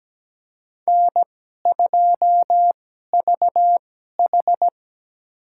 Morse code, heard as a keyed tone, about 17 wpm, spaced as for 12 wpm, 700 Hz. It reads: N2VH
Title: USA Call Signs - 2x Character Spacing - 17wpm